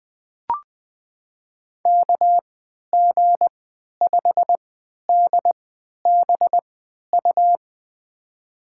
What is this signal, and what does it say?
Morse code, heard as a keyed tone, about 20 wpm, spaced as for 11 wpm, 700 Hz. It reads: KG5DBU
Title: USA Call Signs - 6 Characters - 3x Character Spacing - 20wpm